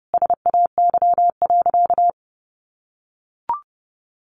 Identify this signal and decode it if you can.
Morse code, heard as a keyed tone, about 30 wpm, 700 Hz. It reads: SAY.
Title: Sentences from Top 1500 Words 30wpm